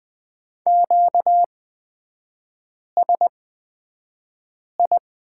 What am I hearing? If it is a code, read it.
Morse code, heard as a keyed tone, about 20 wpm, spaced as for 5 wpm, 700 Hz. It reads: QSI
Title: 236 Lesson 38e - Letter q - Three Characters 20wpm